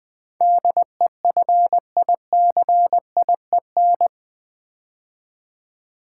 Morse code, heard as a keyed tone, about 20 wpm, 700 Hz. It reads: DEFICIEN
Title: Build A Word - 4 Letters At A Time - 10-Letter Words - 20wpm